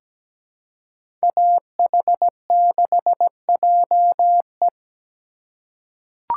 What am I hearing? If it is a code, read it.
Morse code, heard as a keyed tone, about 17 wpm, 700 Hz. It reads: AH6JE
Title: USA Call Signs - 5 Characters - 3x Character Spacing - 17wpm